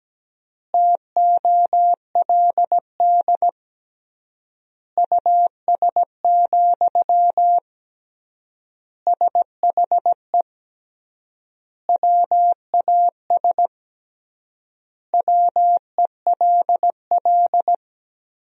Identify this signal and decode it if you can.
Morse code, heard as a keyed tone, about 17 wpm, 700 Hz. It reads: TOLD US, SHE WAS WELL
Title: 3x Word-Spacing - Sentences from Top 400 Words - 17wpm